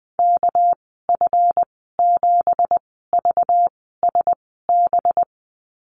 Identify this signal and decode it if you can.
Morse code, heard as a keyed tone, about 19 wpm, spaced as for 14 wpm, 700 Hz. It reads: KF7VSB
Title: 200 Lesson 32g - Letter k - Call Signs 20wpm